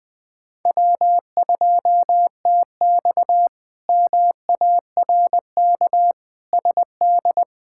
Morse code, heard as a keyed tone, about 20 wpm, 700 Hz. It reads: W2TX MARK SD
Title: K1USN SST QSO Practice - 20wpm